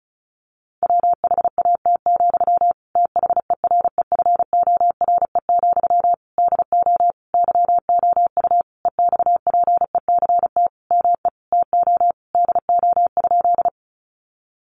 Morse code, heard as a keyed tone, about 34 wpm, 700 Hz. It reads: WHAT, THEREFORE, DO YOU EXPECT ME TO DO?
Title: Sentences from Top 1100 Words 35wpm